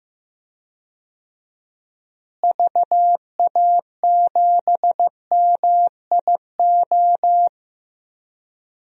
Morse code, heard as a keyed tone, about 15 wpm, 700 Hz. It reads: VA7MIO
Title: DXCC-100 Call Signs - 6 Characters - Repeated 3x - 15wpm